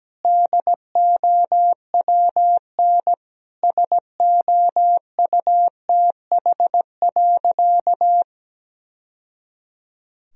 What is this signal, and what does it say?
Morse code, heard as a keyed tone, about 17 wpm, 700 Hz. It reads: DOWN SOUTH.
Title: Sentences from Top 1000 Words - Review Part 2 - 17wpm